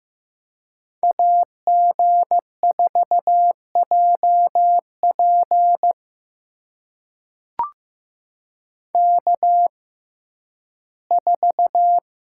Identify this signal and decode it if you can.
Morse code, heard as a keyed tone, about 15 wpm, spaced as for 11 wpm, 700 Hz. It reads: AG4JP K4
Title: USA Call Signs - 5 Characters - 6x Character Spacing - 15wpm